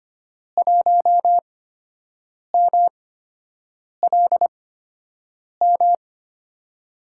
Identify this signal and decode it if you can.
Morse code, heard as a keyed tone, about 25 wpm, spaced as for 7 wpm, 700 Hz. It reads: 1MLM
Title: USA Call Signs - 5 Characters - 8x Character Spacing - 25wpm